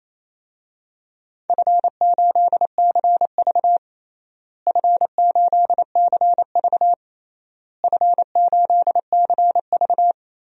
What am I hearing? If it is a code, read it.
Morse code, heard as a keyed tone, about 27 wpm, 700 Hz. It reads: F8CV F8CV F8CV
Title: DXCC-100 Call Signs - 4 Characters - Repeated 3x - 28wpm